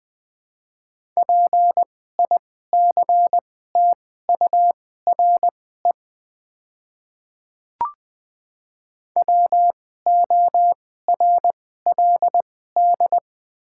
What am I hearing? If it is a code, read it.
Morse code, heard as a keyed tone, about 20 wpm, spaced as for 14 wpm, 700 Hz. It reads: PICTURE WORLD
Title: 2x Character Spacing - Top 200 Words - 20wpm